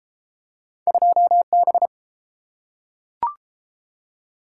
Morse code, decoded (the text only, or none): 2B